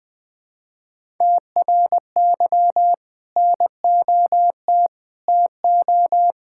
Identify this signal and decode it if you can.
Morse code, heard as a keyed tone, about 21 wpm, 700 Hz. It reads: TRY NOT TO
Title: Sentences with 2-Word Phrases - 20wpm